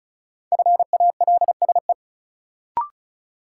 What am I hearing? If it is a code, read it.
Morse code, heard as a keyed tone, about 36 wpm, 700 Hz. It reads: FALSE
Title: Sentences from Top 1600 Words - New Word Drill - 35wpm